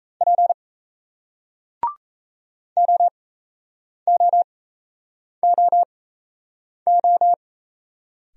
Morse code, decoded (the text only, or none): P OOOO